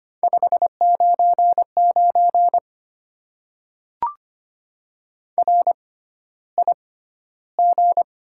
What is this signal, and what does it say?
Morse code, heard as a keyed tone, about 25 wpm, spaced as for 13 wpm, 700 Hz. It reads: 599 RIG
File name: Top 90 QSO Elements - 6x Character Spacing - 25wpm